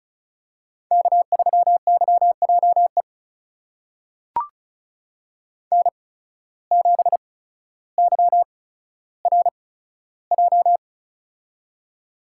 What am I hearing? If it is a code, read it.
Morse code, heard as a keyed tone, about 36 wpm, spaced as for 15 wpm, 700 Hz. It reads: K3YJE N7YRJ
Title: USA Call Signs - 5 Characters - 8x Character Spacing - 35wpm